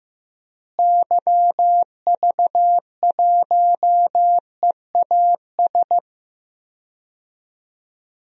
Morse code, heard as a keyed tone, about 15 wpm, 700 Hz. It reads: YV1EAS
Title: DXCC-100 Call Signs - 6 Characters - Repeated 2x - 15wpm